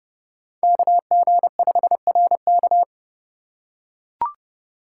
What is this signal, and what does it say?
Morse code, heard as a keyed tone, about 32 wpm, 700 Hz. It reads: KG5RK
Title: USA Call Signs - 5 Characters - 2x Character Spacing - 30wpm